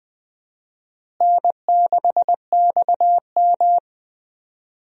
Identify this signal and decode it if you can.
Morse code, heard as a keyed tone, about 20 wpm, 700 Hz. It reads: N6XM
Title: USA Call Signs - 4 Characters - Repeated 2x - 20wpm